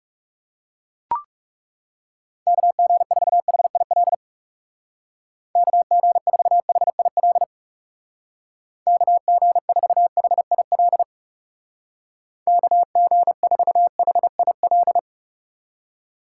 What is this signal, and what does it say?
Morse code, heard as a keyed tone, about 35 wpm, 700 Hz. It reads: KG4HIL KG4HIL KG4HIL KG4HIL
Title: Speed Racing - USA Call Signs 30wpm